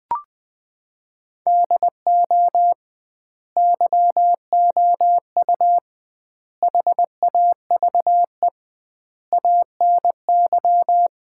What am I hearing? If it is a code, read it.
Morse code, heard as a keyed tone, about 21 wpm, 700 Hz. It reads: DO YOU HAVE ANY
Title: 2x Word-Spacing - Sets of 4 Words - 20wpm